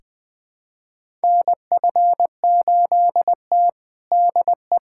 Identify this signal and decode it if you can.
Morse code, heard as a keyed tone, about 20 wpm, 700 Hz. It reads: NF8T DE
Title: USA Callsign DE Callsign - 4 Characters - 20wpm